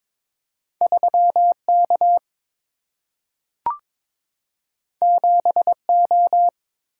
Morse code, heard as a keyed tone, about 22 wpm, 700 Hz. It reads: 3K 7O
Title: Single-Letter Single-Number 22wpm